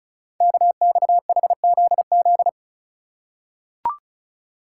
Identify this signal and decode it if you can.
Morse code, heard as a keyed tone, about 36 wpm, 700 Hz. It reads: KXHZZ
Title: Code Groups - 5 Letters - 35wpm